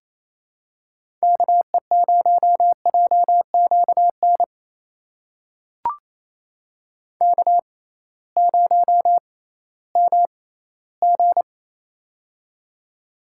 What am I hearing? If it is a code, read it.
Morse code, heard as a keyed tone, about 28 wpm, spaced as for 16 wpm, 700 Hz. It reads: KE0JQN K0MG
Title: USA Call Signs - 6x Character Spacing - 28wpm